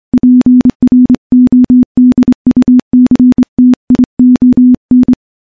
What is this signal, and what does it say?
Morse code, heard as a keyed tone, about 26 wpm, 250 Hz. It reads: PRODUCTION
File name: morse-training-25WPM